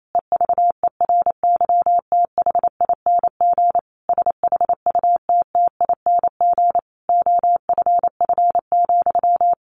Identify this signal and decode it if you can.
Morse code, heard as a keyed tone, about 28 wpm, 700 Hz. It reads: EVERYTHING SHUTTING OFF,
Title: A Sound Of Thunder 28wpm